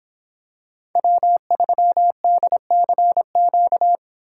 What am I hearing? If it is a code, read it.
Morse code, heard as a keyed tone, about 24 wpm, 700 Hz. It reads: W3DCQ